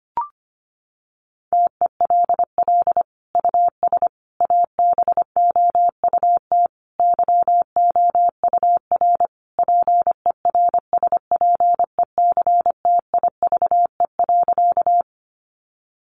Morse code, heard as a keyed tone, about 25 wpm, 700 Hz. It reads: TELL US ABOUT YOUR PERSPECTIVE.
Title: Sentences from Top 1200 Words 25wpm